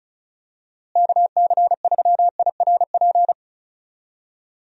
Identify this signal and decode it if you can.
Morse code, heard as a keyed tone, about 36 wpm, 700 Hz. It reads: KC3IRP
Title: USA Call Signs - 6 Characters - Repeated 3x - 35wpm